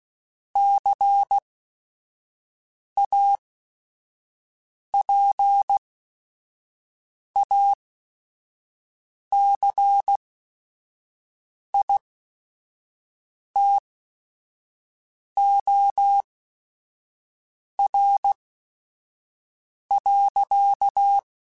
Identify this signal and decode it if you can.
Morse code, heard as a keyed tone, about 16 wpm, spaced as for 5 wpm, 800 Hz. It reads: CAPACITOR.
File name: Technical Subjects / Technical Subjects-mp3s / Reactance @5